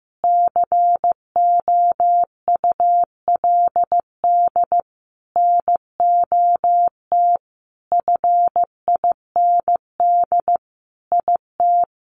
Morse code, heard as a keyed tone, about 15 wpm, 700 Hz. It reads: COULD NOT FIND IT
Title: Sentences from Top 1400 Words 15wpm